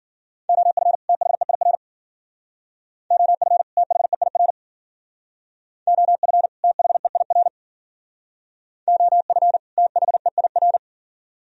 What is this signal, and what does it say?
Morse code, heard as a keyed tone, about 48 wpm, 700 Hz. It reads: OF THEIR OF THEIR OF THEIR OF THEIR
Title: Speed-Racing - Sets of 2 Words 40wpm